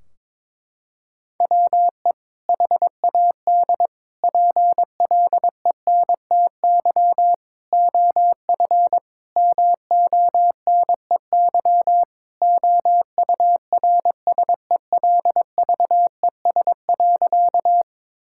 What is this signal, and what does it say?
Morse code, heard as a keyed tone, about 22 wpm, 700 Hz. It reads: WE HAD PLENTY OF MONEY OURSELVES.
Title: Sentences from Top 1600 Words 22wpm